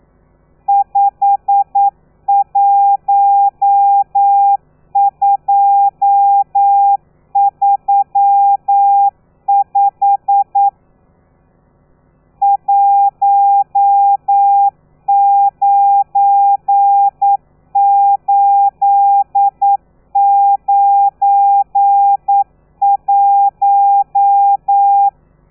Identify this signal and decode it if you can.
Morse code, heard as a keyed tone, about 9 wpm, 800 Hz. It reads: 51235 19891
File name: telegrafia / i2tfj / Lez12 40